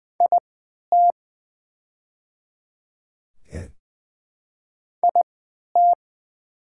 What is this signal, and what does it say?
Morse code, heard as a keyed tone, about 20 wpm, spaced as for 11 wpm, 700 Hz. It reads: IT IT
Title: 3x Character Spacing - Top 200 Words - 20wpm